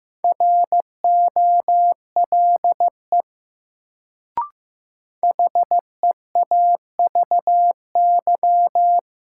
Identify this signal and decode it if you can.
Morse code, heard as a keyed tone, about 15 wpm, 700 Hz. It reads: ROLE HEAVY
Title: Sentences from Top 1000 Words - Review Part 6 - New Word Drill - 15wpm